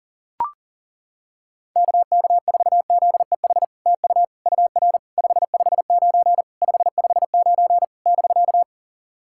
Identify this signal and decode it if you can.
Morse code, heard as a keyed tone, about 40 wpm, 700 Hz. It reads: KK4ZEH TU UR 559 559 <BK>
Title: SOTA QSO Practice 40wpm